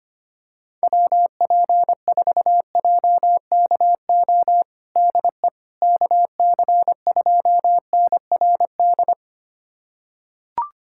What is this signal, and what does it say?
Morse code, heard as a keyed tone, about 25 wpm, 700 Hz. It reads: WP4JKO DE KC2NRD
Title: USA Callsign DE Callsign 25wpm